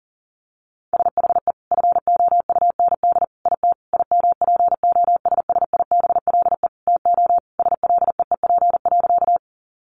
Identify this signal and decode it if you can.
Morse code, heard as a keyed tone, about 40 wpm, 700 Hz. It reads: SHE FOUND IT IMPOSSIBLE TO SLEEP.
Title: Sentences from Top 1100 Words 40wpm